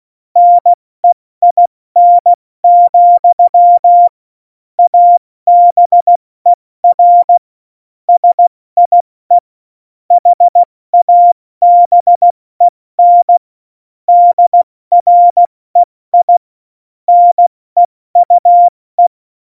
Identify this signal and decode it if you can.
Morse code, heard as a keyed tone, about 16 wpm, 700 Hz. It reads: NEIN, ABER SIE HABEN DREI NEUE
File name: Morsekurs / Witze / Witz 08. Januar 2019, 70 F80